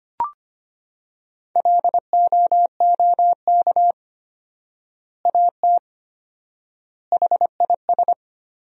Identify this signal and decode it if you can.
Morse code, heard as a keyed tone, about 25 wpm, 700 Hz. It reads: LOOK AT HIS
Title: 4x Word-Spacing - Sentences from Top 300 Words - 25wpm